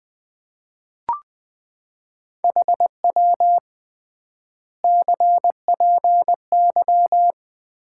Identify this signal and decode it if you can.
Morse code, heard as a keyed tone, about 20 wpm, 700 Hz. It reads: HW CPY
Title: Common QSO Phrases - 3x Word-Spacing - 20wpm